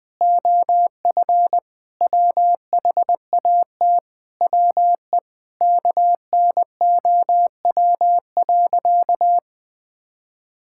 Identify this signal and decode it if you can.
Morse code, heard as a keyed tone, about 20 wpm, 700 Hz. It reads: OF WHAT WE KNOW.
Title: Sentences with 8-Word Phrases - 20wpm